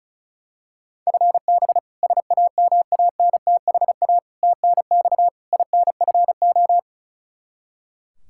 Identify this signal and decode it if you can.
Morse code, heard as a keyed tone, about 34 wpm, 700 Hz. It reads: FB SAMANTHA TNX INFO
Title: K1USN SST QSO Practice 35wpm